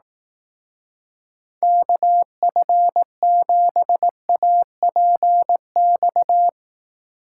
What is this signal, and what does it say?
Morse code, heard as a keyed tone, about 18 wpm, 700 Hz. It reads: KF7APX
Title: Speed Racing - USA 2x3 Call Signs 15wpm